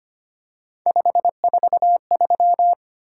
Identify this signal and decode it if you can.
Morse code, heard as a keyed tone, about 25 wpm, 700 Hz. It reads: 543
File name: Three Letter-Number Combinations - 25wpm